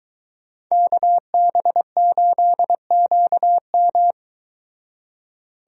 Morse code, heard as a keyed tone, about 23 wpm, 700 Hz. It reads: KB8QM